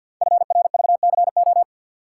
Morse code, heard as a keyed tone, about 49 wpm, 700 Hz. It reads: FRVCO
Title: Code Groups - 5 Letters - 50wpm